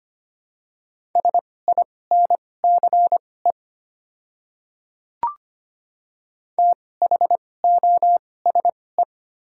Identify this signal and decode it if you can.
Morse code, heard as a keyed tone, about 25 wpm, spaced as for 18 wpm, 690 Hz. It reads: SINCE THOSE
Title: 2x Character Spacing - Top 300 Words - 25wpm